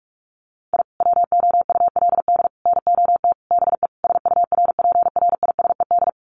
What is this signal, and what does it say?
Morse code, heard as a keyed tone, about 44 wpm, 700 Hz. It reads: I WOULD NOT BE SURPRISED
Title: Sets of 5 Words Encore2 45wpm